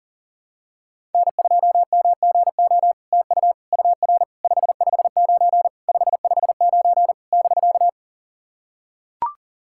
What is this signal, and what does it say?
Morse code, heard as a keyed tone, about 40 wpm, 700 Hz. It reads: N2MGO TU UR 559 559 <BK>
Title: SOTA QSO Practice 40wpm